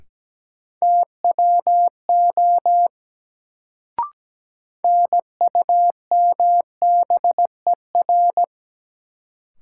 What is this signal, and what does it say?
Morse code, heard as a keyed tone, about 17 wpm, 700 Hz. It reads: TWO NUMBER
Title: Sentences from Top 100 Words - New Word Drill - 17wpm